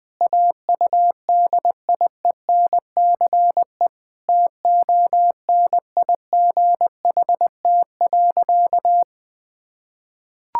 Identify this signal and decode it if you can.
Morse code, heard as a keyed tone, about 20 wpm, 700 Hz. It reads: AUDIENCE TONIGHT.
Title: Sentences from Top 1000 Words - Review Part 5 - 20wpm